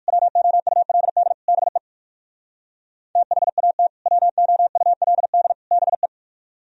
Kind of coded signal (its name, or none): Morse code